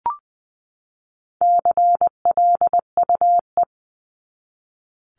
Morse code, heard as a keyed tone, about 20 wpm, 700 Hz. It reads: CLUE
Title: Top 1751-2000 Words 20wpm